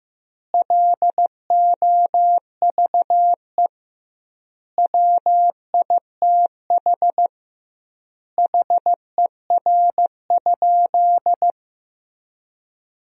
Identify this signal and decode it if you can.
Morse code, heard as a keyed tone, about 15 wpm, 700 Hz. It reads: LOVE WITH HER?
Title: 2x Word-Spacing - Sentences from Top 400 Words - 15wpm